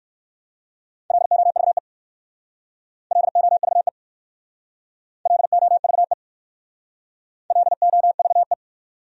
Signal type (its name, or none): Morse code